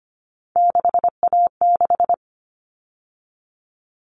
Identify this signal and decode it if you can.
Morse code, heard as a keyed tone, about 25 wpm, 700 Hz. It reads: 6A6
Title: All Cut Numbers - 3-Digit 25wpm